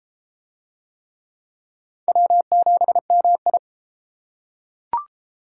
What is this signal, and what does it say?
Morse code, heard as a keyed tone, about 30 wpm, 700 Hz. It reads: W7MI